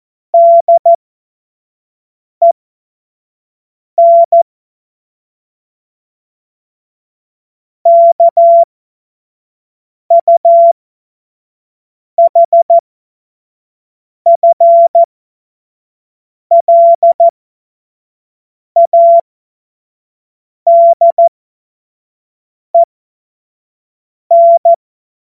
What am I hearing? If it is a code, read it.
Morse code, heard as a keyed tone, about 14 wpm, spaced as for 5 wpm, 680 Hz. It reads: DEN KUHFLADEN
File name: Witz-09-September-2025-25-F70